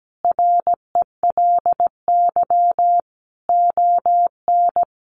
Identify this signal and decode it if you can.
Morse code, heard as a keyed tone, about 17 wpm, 700 Hz. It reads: RELY ON
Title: Sentences from Top 1200 Words 17wpm